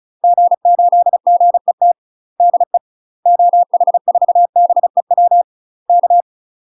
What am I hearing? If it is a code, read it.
Morse code, heard as a keyed tone, about 34 wpm, 700 Hz. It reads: G8GET DE OH4BEW K
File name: UK MASTER QSOs TRAINING / 35WPM / UK MASTER QSOs 0037